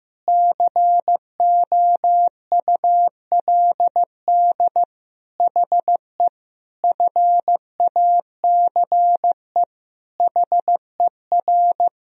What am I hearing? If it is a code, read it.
Morse code, heard as a keyed tone, about 15 wpm, 700 Hz. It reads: COULD HE FACE HER
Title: Sentences from Top 1000 Words - Review Part 2 - 15wpm